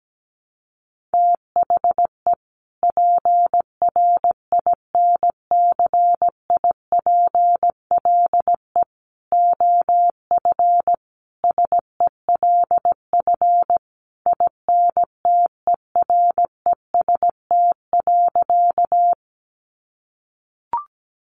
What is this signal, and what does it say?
Morse code, heard as a keyed tone, about 17 wpm, 700 Hz. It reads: THE PRINCIPLE OF SELF INTEREST.
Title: Sentences from Top 1400 Words 17wpm